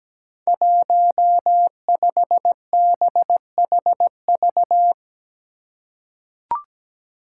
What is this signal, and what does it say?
Morse code, heard as a keyed tone, about 17 wpm, 700 Hz. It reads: 15BHV
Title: Five Challenging Letter-Number Combinations 17wpm